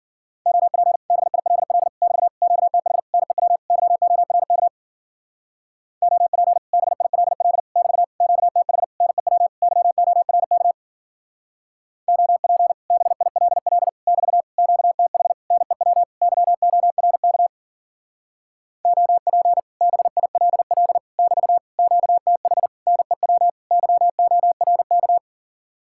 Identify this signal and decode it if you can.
Morse code, heard as a keyed tone, about 50 wpm, 700 Hz. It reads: OP BILL <BT> QTH NEW YORK OP BILL <BT> QTH NEW YORK OP BILL <BT> QTH NEW YORK OP BILL <BT> QTH NEW YORK